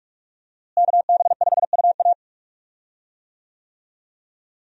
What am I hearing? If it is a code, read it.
Morse code, heard as a keyed tone, about 46 wpm, 700 Hz. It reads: KB5UA
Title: USA Call Signs - 5 Characters - Repeated 3x - 45wpm